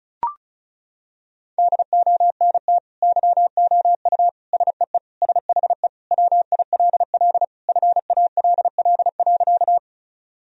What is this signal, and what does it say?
Morse code, heard as a keyed tone, about 36 wpm, 700 Hz. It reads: DONT YOU SEE SHE WILL FALL.
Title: Sentences from Top 1000 Words - Review Part 2 - 35wpm